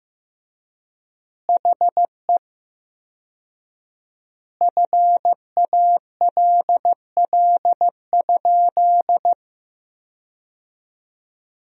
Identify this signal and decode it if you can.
Morse code, heard as a keyed tone, about 15 wpm, 700 Hz. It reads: HE FALL?
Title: 4x Word-Spacing - Sentences from Top 500 Words - 15wpm